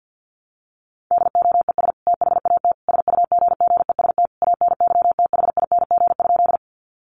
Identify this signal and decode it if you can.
Morse code, heard as a keyed tone, about 50 wpm, 700 Hz. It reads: DOES THAT SUGGEST ANYTHING?
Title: Sentences from Top 1300 Words 50wpm